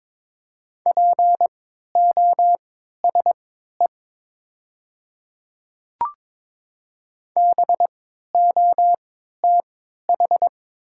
Morse code, heard as a keyed tone, about 22 wpm, spaced as for 12 wpm, 690 Hz. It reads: POSE BOTH
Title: 3x Character Spacing - Top 500 Words - 22wpm